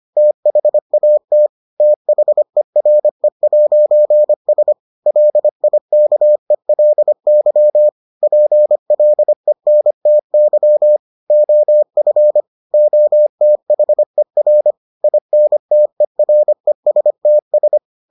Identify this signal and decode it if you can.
Morse code, heard as a keyed tone, about 25 wpm, 600 Hz. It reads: THAT THERE'S LIKELY PLENTY OF OTHER INTERESTS